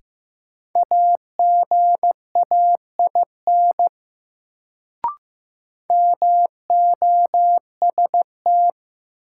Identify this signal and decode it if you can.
Morse code, heard as a keyed tone, about 15 wpm, 700 Hz. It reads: AGAIN MOST
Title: Sentences from Top 200 Words - New Word Drill - 15wpm